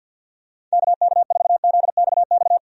Morse code, heard as a keyed tone, about 46 wpm, 700 Hz. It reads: KK4ZXX
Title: USA Call Signs - 6 Characters - 50wpm